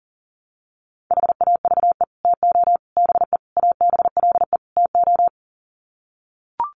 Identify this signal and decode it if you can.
Morse code, heard as a keyed tone, about 40 wpm, 700 Hz. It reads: HAVE TO BE ABLE TO
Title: Sets of 5 Words 40wpm